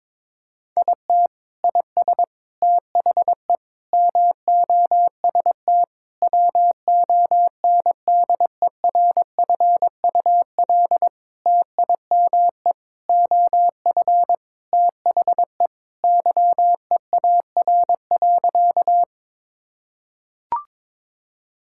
Morse code, heard as a keyed tone, about 22 wpm, 700 Hz. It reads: IT IS THE MOST WONDERFUL TIME OF THE YEAR.
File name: practice / 22wpm / Sentences with 8-Word Phrases - 22wpm